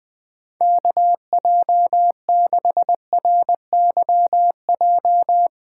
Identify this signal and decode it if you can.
Morse code, heard as a keyed tone, about 20 wpm, 700 Hz. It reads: KJ6RYJ